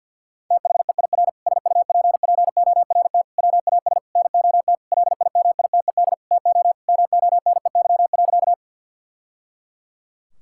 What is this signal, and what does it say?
Morse code, heard as a keyed tone, about 50 wpm, 700 Hz. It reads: THEIR SUPPORT WAS NOT LIMITED TO MONEY.
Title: Sentences from Top 1000 Words - Review Part 6 - 50wpm